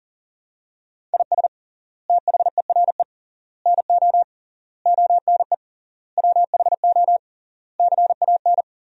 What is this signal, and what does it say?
Morse code, heard as a keyed tone, about 40 wpm, 700 Hz. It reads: IS THERE NO ONE WHO CAN
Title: 3x Word-Spacing - Sentences from Top 200 Words - 40wpm